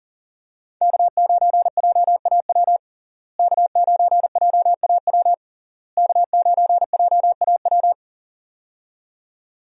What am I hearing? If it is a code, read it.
Morse code, heard as a keyed tone, about 40 wpm, 700 Hz. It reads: K9JAW K9JAW K9JAW
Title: USA Call Signs - 5 Characters - Repeated 3x - 40wpm